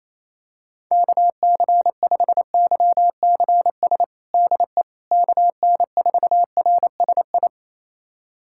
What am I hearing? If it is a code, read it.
Morse code, heard as a keyed tone, about 28 wpm, 700 Hz. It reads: KC5YCS DE KN4RSI
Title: USA Callsign DE Callsign - 6 Characters - 28wpm